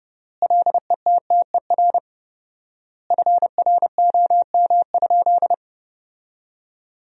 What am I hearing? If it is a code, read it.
Morse code, heard as a keyed tone, about 31 wpm, 700 Hz. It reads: LETTER FROM?
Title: 4x Word-Spacing - Sentences from Top 300 Words - 30wpm